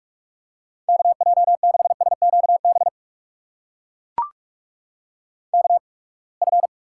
Morse code, heard as a keyed tone, about 44 wpm, spaced as for 31 wpm, 700 Hz. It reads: KJ6SQB KF
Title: USA Call Signs - 8x Character Spacing - 45wpm